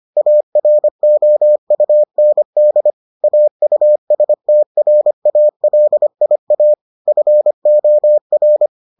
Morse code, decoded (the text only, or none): AROUND AUSTRALIA FOR